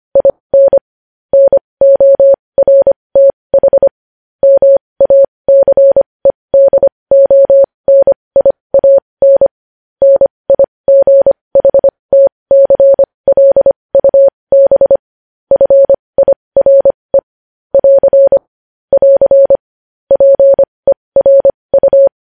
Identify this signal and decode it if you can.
Morse code, heard as a keyed tone, about 25 wpm, spaced as for 20 wpm, 550 Hz. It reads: IN NORTH MACEDONIAN NIGHTCLUB FIRE <AR> <AR> PERU
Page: News Headlines in Morse Code at 20 WPM